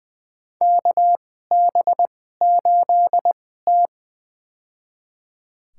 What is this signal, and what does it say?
Morse code, heard as a keyed tone, about 20 wpm, spaced as for 15 wpm, 700 Hz. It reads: KB8T
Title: USA Call Signs - 4 Characters - 2x Character Spacing - 20wpm